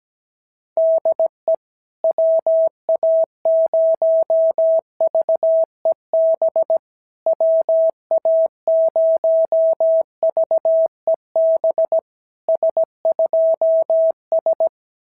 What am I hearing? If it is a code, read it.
Morse code, heard as a keyed tone, about 17 wpm, 650 Hz. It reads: DE WA0VEB WA0VEB S2S
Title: SOTA QSO Practice 17wpm